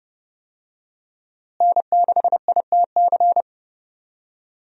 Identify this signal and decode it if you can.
Morse code, heard as a keyed tone, about 31 wpm, 700 Hz. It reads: N6ITC
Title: USA Call Signs - 5 Characters - Repeated 2x - 30wpm